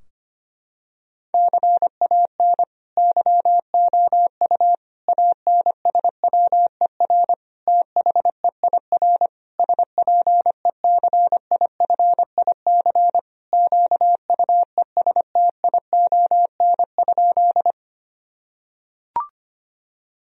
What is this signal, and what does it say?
Morse code, heard as a keyed tone, about 25 wpm, 700 Hz. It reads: CAN YOU ANSWER THEIR SPECIFIC QUESTION?
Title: Sentences from Top 1000 Words - Review Part 3 - 25wpm